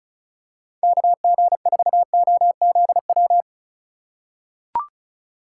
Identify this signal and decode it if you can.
Morse code, heard as a keyed tone, about 36 wpm, 700 Hz. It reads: KG4OZW
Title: USA Call Signs - 6 Characters - 2x Character Spacing - 35wpm